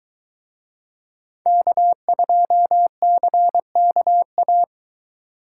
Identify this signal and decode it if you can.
Morse code, heard as a keyed tone, about 24 wpm, 700 Hz. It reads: K2CKA